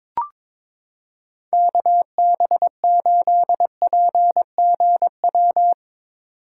Morse code, heard as a keyed tone, about 22 wpm, 700 Hz. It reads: KB8PGW
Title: K1USN SST QSO Practice - 22wpm